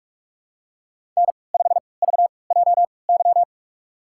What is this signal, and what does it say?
Morse code, heard as a keyed tone, about 44 wpm, spaced as for 25 wpm, 700 Hz. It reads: N5VJY
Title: USA Call Signs - 3x Character Spacing - 45wpm